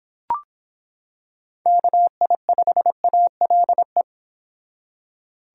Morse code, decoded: KI5ALE

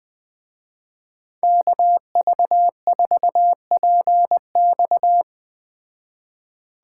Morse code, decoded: KV4PX

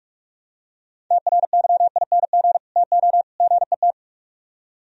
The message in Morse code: TRYING TO GET

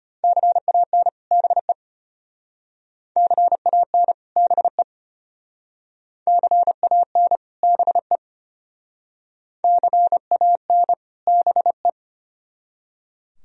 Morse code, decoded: CAN BE CAN BE CAN BE CAN BE

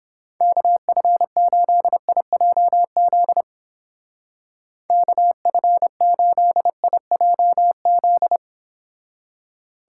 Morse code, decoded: KF8IJZ KF8IJZ

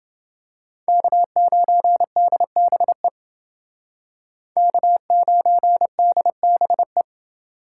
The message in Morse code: K9DBE K9DBE

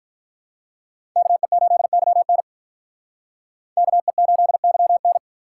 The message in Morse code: KE8YN KE8YN

